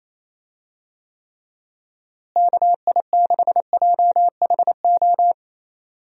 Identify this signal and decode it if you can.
Morse code, heard as a keyed tone, about 28 wpm, 700 Hz. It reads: KI6JHO
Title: USA Call Signs - 6 Characters - Repeated 2x - 28wpm